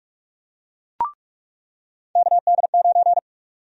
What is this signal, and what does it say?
Morse code, heard as a keyed tone, about 44 wpm, 700 Hz. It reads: KD9
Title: USA Call Sign Prefixes - 45wpm